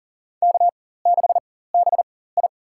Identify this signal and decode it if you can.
Morse code, heard as a keyed tone, about 40 wpm, spaced as for 19 wpm, 700 Hz. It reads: K6BI
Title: USA Call Signs - 4 Characters - 4x Character Spacing - 40wpm